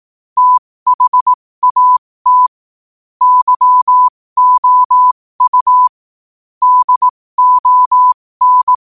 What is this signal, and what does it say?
Morse code, heard as a keyed tone, about 18 wpm, spaced as for 15 wpm, 1000 Hz. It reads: THAT YOU DON
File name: qotdfiles / 15wpm